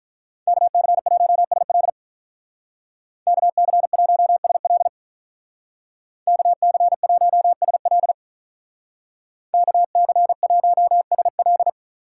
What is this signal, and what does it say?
Morse code, heard as a keyed tone, about 44 wpm, 700 Hz. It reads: KC1SL KC1SL KC1SL KC1SL